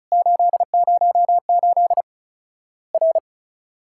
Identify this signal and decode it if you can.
Morse code, heard as a keyed tone, about 36 wpm, 700 Hz. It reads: 808 R
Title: Worked All Europe DX Contest - QTC Exchange - 35wpm